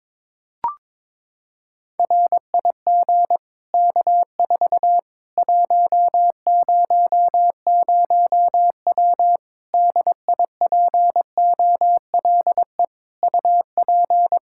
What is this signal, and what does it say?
Morse code, heard as a keyed tone, about 22 wpm, 700 Hz. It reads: RIG K4 100W DIPOLE UP
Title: Practice QSO - Exchange 2 - Info - 22wpm